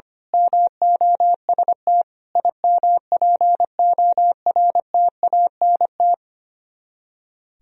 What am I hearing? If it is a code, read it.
Morse code, heard as a keyed tone, about 24 wpm, 700 Hz. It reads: MOST IMPORTANT
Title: Speed-Racing - Sets of 4 Words 25wpm